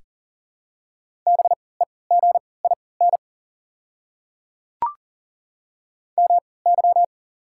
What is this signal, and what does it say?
Morse code, heard as a keyed tone, about 40 wpm, spaced as for 23 wpm, 700 Hz. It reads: BEGIN MY
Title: 3x Character Spacing - Top 500 Words - 40wpm